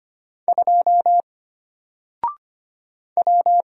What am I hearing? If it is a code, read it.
Morse code, heard as a keyed tone, about 25 wpm, 700 Hz. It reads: 2 W
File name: Single-Letter-Number-25wpm